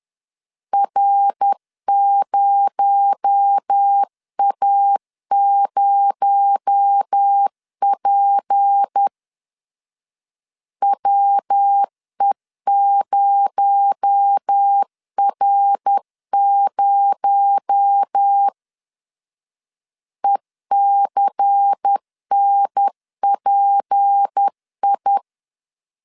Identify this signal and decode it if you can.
Morse code, heard as a keyed tone, about 10 wpm, 800 Hz. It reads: R0A0P WE0R0 ECNPI
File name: LEZIONI / 6 PKX0 / 6 PKX0 10wpm 2707